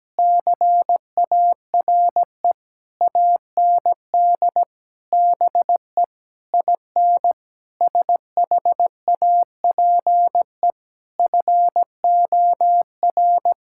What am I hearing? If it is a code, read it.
Morse code, heard as a keyed tone, about 17 wpm, 700 Hz. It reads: CARE AND BE IN SHAPE FOR
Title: Sentences from Top 1000 Words - Review Part 6 - 17wpm